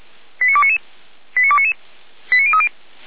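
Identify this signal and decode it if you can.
Five-tone selective call (ZVEI), heard as quick runs of five tones, tones of about 80 ms. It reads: 89299 89299 89030
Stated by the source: Five-tone ZVEI selective call, received by SDR, fire brigade Orlamünde.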